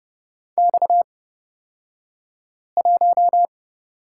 Morse code, heard as a keyed tone, about 30 wpm, spaced as for 5 wpm, 700 Hz. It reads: X1
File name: Lesson 37d - Letter x - Two Characters 30wpm